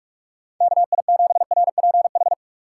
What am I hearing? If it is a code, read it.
Morse code, heard as a keyed tone, about 46 wpm, 700 Hz. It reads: KI7RPH
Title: USA Call Signs - 6 Characters - 2x Character Spacing - 45wpm